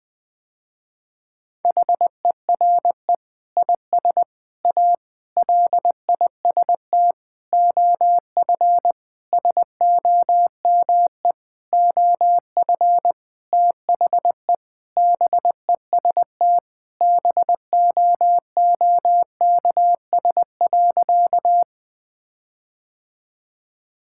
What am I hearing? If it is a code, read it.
Morse code, heard as a keyed tone, about 20 wpm, 700 Hz. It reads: HERE IS A LIST OF SOME OF THE BEST BOOKS.
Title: Sentences with 8-Word Phrases - Spoken Beginning - 20wpm